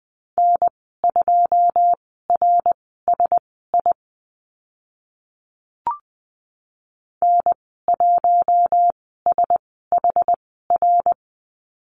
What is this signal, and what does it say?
Morse code, heard as a keyed tone, about 20 wpm, spaced as for 15 wpm, 700 Hz. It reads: N2RSI N1SHR
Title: 082 Lesson 14g - Number 2 - Call Signs 20wpm